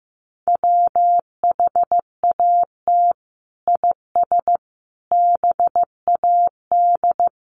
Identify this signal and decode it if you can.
Morse code, heard as a keyed tone, about 15 wpm, 700 Hz. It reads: WHAT IS BAD
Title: Sentences from Top 700 Words 15wpm